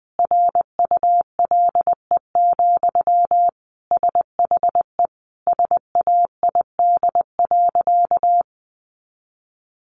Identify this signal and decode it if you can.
Morse code, heard as a keyed tone, about 20 wpm, 700 Hz. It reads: RULE, SHE SAID.
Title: Sentences from Top 400 Words 20wpm